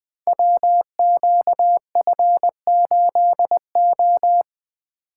Morse code, heard as a keyed tone, about 20 wpm, 700 Hz. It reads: WQF8O
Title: Code Groups - 5 Letter:Numbers - 20wpm